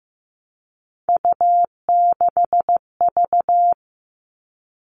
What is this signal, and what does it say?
Morse code, heard as a keyed tone, about 15 wpm, 700 Hz. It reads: U6V
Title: All Cut Numbers - 3-Digit 15wpm